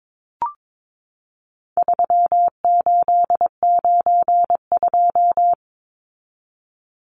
Morse code, heard as a keyed tone, about 22 wpm, 700 Hz. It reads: 3892
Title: Numbers - 4-Digit 22wpm